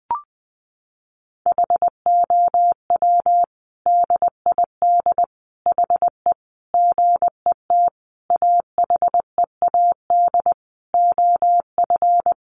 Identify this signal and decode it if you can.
Morse code, heard as a keyed tone, about 20 wpm, 700 Hz. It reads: HOW DID HE GET AHEAD OF
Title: Sentences from Top 1100 Words 20wpm